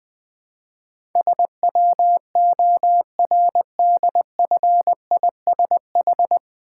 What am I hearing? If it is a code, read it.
Morse code, heard as a keyed tone, about 20 wpm, 700 Hz. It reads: SWORDFISH
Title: Difficult Words with Hints - 20wpm